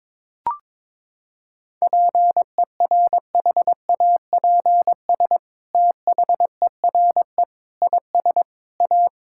Morse code, heard as a keyed tone, about 21 wpm, 700 Hz. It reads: PERHAPS THERE IS A
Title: Sentences from Top 1000 Words - Review Part 5 - 22wpm